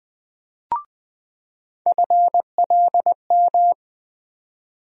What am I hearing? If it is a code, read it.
Morse code, heard as a keyed tone, about 20 wpm, 700 Hz. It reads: FLM
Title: USA Call Sign Postfixes - 3 Letters 20wpm